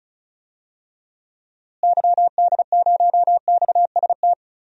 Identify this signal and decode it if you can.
Morse code, heard as a keyed tone, about 36 wpm, 700 Hz. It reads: YD0XST
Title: DXCC-100 Call Signs - 6 Characters - Repeated 3x - 35wpm